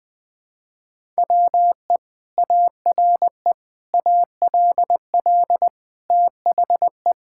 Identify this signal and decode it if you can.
Morse code, heard as a keyed tone, about 20 wpm, 700 Hz. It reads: WE ARE ALL THE
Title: Sentences with 2-Word Phrases - 20wpm